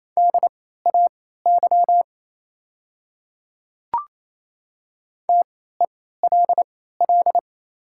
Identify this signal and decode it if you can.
Morse code, heard as a keyed tone, about 28 wpm, spaced as for 16 wpm, 700 Hz. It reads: DAY TELL
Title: 3x Character Spacing - Top 200 Words - 28wpm